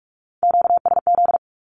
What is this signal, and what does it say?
Morse code, heard as a keyed tone, about 46 wpm, 700 Hz. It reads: QSZ
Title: Q-Codes with Spoken Meaning - 45wpm